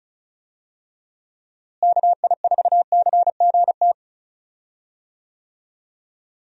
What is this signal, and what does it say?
Morse code, heard as a keyed tone, about 34 wpm, 700 Hz. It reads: KI4CGT
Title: USA Call Signs - 6 Characters - Repeated 2x - 35wpm